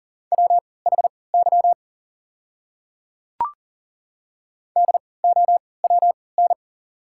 Morse code, heard as a keyed tone, about 40 wpm, spaced as for 23 wpm, 700 Hz. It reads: WHY DOWN
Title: 3x Character Spacing - Top 200 Words - 40wpm